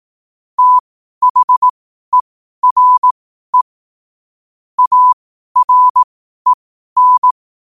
Morse code, heard as a keyed tone, about 18 wpm, spaced as for 12 wpm, 1000 Hz. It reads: THERE AREN
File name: 2026-01-07-12wpm